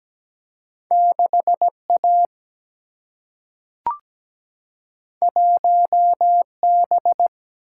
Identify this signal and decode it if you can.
Morse code, heard as a keyed tone, about 17 wpm, 700 Hz. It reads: 6 A 1 B
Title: Single-Letter Single-Number 17wpm